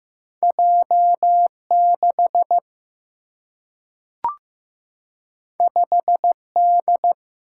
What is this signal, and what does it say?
Morse code, heard as a keyed tone, about 15 wpm, 700 Hz. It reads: J 6 5 D
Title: Single-Letter Single-Number 15wpm